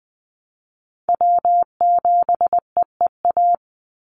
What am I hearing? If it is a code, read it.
Morse code, heard as a keyed tone, about 20 wpm, 700 Hz. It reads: W7EEA
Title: USA Call Signs 20wpm